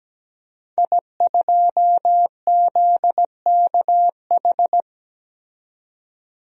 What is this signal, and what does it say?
Morse code, heard as a keyed tone, about 17 wpm, 700 Hz. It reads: I2ZKH
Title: DXCC-100 Call Signs - 5 Characters - 17wpm